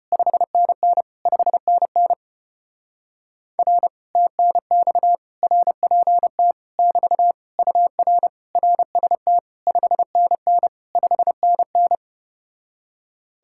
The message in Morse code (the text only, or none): 5NN 5NN R TNX RPT <BT> UR RST 5NN 5NN